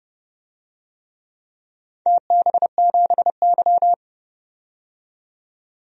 Morse code, heard as a keyed tone, about 30 wpm, 700 Hz. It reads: TB7Y
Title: DXCC-100 Call Signs - 4 Characters - 30wpm